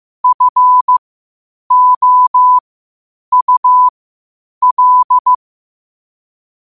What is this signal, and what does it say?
Morse code, heard as a keyed tone, about 15 wpm, spaced as for 8 wpm, 1000 Hz. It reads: FOUL
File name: qotdfiles / 8wpm